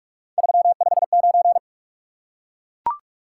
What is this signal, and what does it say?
Morse code, heard as a keyed tone, about 43 wpm, 700 Hz. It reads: 359
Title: Top 180 QSO Elements 45wpm